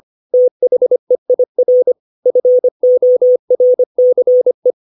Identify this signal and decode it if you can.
Morse code, heard as a keyed tone, about 25 wpm, 500 Hz. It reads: THEIR FORCE
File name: 632 words / 632 Words 25WPM